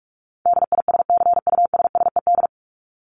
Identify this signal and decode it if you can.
Morse code, heard as a keyed tone, about 44 wpm, 700 Hz. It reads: DISCUSSED
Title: ED Suffix - Top 100 Words with Base Words 45wpm